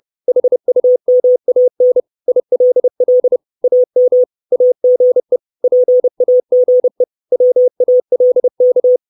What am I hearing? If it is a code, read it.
Morse code, heard as a keyed tone, about 30 wpm, 500 Hz. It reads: HUMAN ILL AM AGE PAGE WALK
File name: 632 Words 30WPM